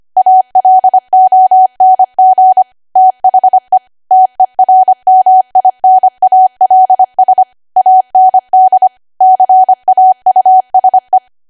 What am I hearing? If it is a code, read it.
Morse code, heard as a keyed tone, about 25 wpm, 750 Hz. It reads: ALONG THE TERMINALS AND CAUSE
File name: Archive / 25 WPM / 25WPM